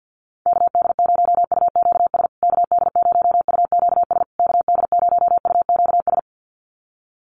Morse code, heard as a keyed tone, about 50 wpm, 700 Hz. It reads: KD0UQS KD0UQS KD0UQS
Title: USA Call Signs Repeated 3x 50wpm